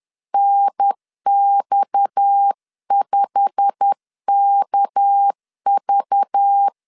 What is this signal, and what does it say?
Morse code, heard as a keyed tone, about 10 wpm, 800 Hz. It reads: NX5KV
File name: CW / DET / 1 9 1500C S A 10wpm 4511